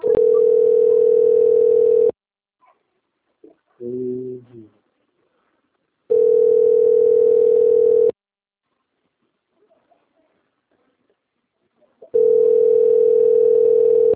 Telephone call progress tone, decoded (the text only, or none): ringback tone